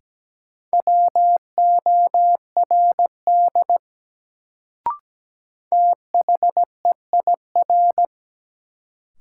Morse code, heard as a keyed tone, about 17 wpm, 700 Hz. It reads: WORD THEIR
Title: Sentences from Top 300 Words - New Word Drill - 17wpm